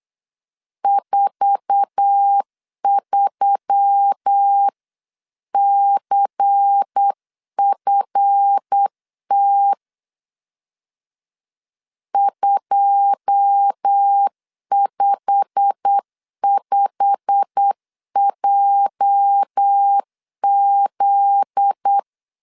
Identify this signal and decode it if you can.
Morse code, heard as a keyed tone, about 8 wpm, 800 Hz. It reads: 43CFT 255JZ
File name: CW / DET / 1 9 1500C S C 8wpm 5653